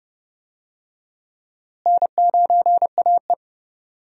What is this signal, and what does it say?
Morse code, heard as a keyed tone, about 30 wpm, 700 Hz. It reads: N9AE
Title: USA Call Signs - 4 Characters - Repeated 2x - 30wpm